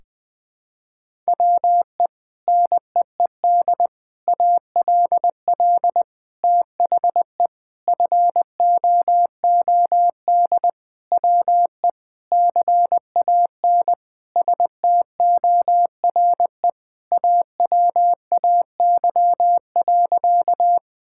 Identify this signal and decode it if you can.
Morse code, heard as a keyed tone, about 20 wpm, 700 Hz. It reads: WE NEED ALL THE FOOD WE CAN STORE AWAY.
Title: Sentences from Top 1000 Words - Review Part 5 - 20wpm